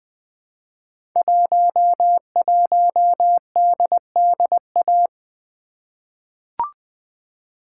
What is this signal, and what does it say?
Morse code, heard as a keyed tone, about 20 wpm, 700 Hz. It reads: 11DDA
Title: Code Groups - 5 Letter:Numbers - 20wpm